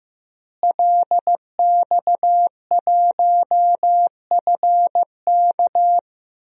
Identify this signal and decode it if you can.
Morse code, heard as a keyed tone, about 15 wpm, 700 Hz. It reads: LX1FK
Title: DXCC-100 Call Signs - 5 Characters - Repeated 2x - 15wpm